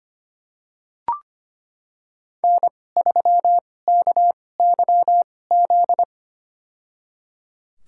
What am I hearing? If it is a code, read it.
Morse code, heard as a keyed tone, about 25 wpm, spaced as for 18 wpm, 700 Hz. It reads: N3KYZ
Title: USA Call Signs - 2x Character Spacing - 25wpm